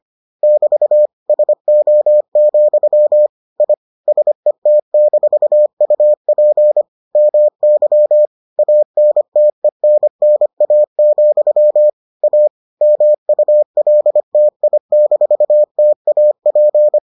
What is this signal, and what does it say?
Morse code, heard as a keyed tone, about 25 wpm, 600 Hz. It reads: <BT> SO, I SET-UP MY ANTENNA, A MULTI-TAP